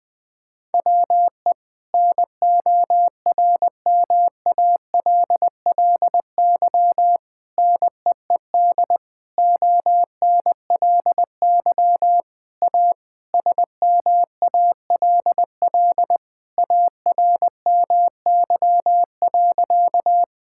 Morse code, heard as a keyed tone, about 21 wpm, 700 Hz. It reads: WE NORMALLY NEED ONLY A SMALL ARMY.
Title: Sentences from Top 1000 Words - Review Part 6 - 20wpm